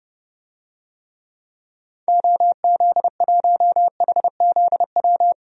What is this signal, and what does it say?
Morse code, heard as a keyed tone, about 31 wpm, 700 Hz. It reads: OZ1HZW
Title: DXCC-100 Call Signs - 6 Characters - 30wpm